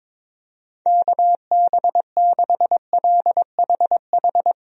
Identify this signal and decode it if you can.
Morse code, heard as a keyed tone, about 22 wpm, 700 Hz. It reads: KB6LHH
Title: USA Call Signs - 6 Characters - 22wpm